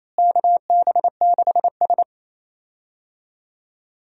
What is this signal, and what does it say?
Morse code, heard as a keyed tone, about 27 wpm, 700 Hz. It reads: KB6S
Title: USA Call Signs - 4 Characters - Repeated 3x - 28wpm